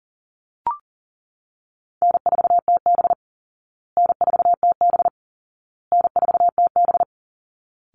Morse code, heard as a keyed tone, about 40 wpm, 700 Hz. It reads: N4TB N4TB N4TB
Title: Easy Call Signs Repeated 3x - 40wpm